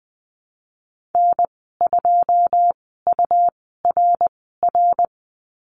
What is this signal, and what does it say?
Morse code, heard as a keyed tone, about 20 wpm, spaced as for 15 wpm, 700 Hz. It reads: N2URR
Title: 173 Lesson 28g - Number 9 - Call Signs 20wpm